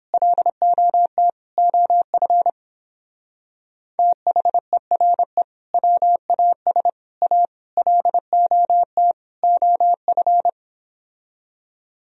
Morse code, heard as a keyed tone, about 31 wpm, 700 Hz. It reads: LOT OF THERE WAS A LOT OF